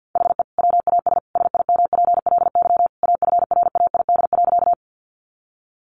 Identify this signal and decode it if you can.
Morse code, heard as a keyed tone, about 50 wpm, 700 Hz. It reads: HE WAS SIMPLY AFRAID.
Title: Sentences from Top 1500 Words 50wpm